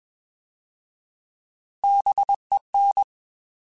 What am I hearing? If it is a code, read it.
Morse code, heard as a keyed tone, about 21 wpm, 800 Hz. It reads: BEN
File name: Alphabetical / Alphabetical mp3s / Bs @20